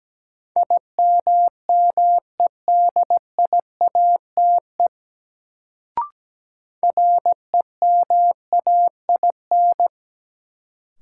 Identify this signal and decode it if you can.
Morse code, heard as a keyed tone, about 17 wpm, 700 Hz. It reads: IMMEDIATE REMAIN
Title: Sentences from Top 1200 Words - New Word Drill - 17wpm